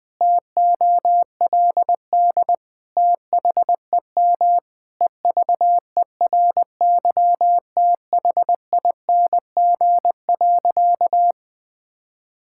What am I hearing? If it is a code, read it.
Morse code, heard as a keyed tone, about 20 wpm, 700 Hz. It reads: TOLD THEM EVERYTHING.
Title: Sentences from Top 1000 Words - Review Part 5 - 20wpm